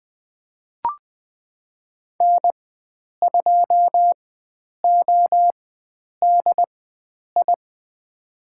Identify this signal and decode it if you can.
Morse code, heard as a keyed tone, about 21 wpm, spaced as for 9 wpm, 700 Hz. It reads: N2ODI
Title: USA Call Signs - 5 Characters - 4x Character Spacing - 20wpm